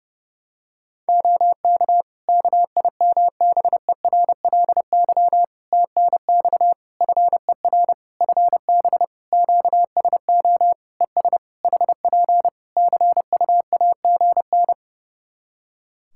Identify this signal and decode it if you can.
Morse code, heard as a keyed tone, about 30 wpm, 700 Hz. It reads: OK KIMBERLY TNX FER FB QSO ES HP CUAGN
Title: Practice QSO - Exchange 4 - Ending - 30wpm